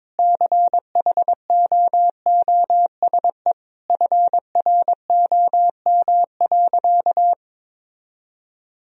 Morse code, heard as a keyed tone, about 21 wpm, 700 Hz. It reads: CHOOSE FROM.
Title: Sentences from Top 1000 Words - Review Part 4 - 22wpm